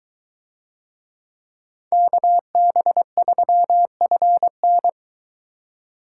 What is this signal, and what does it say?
Morse code, heard as a keyed tone, about 22 wpm, 700 Hz. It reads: KB3FN